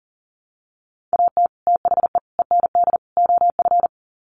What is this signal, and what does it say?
Morse code, heard as a keyed tone, about 40 wpm, 700 Hz. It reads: AT THE END OF
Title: Sets of 4 Words 40wpm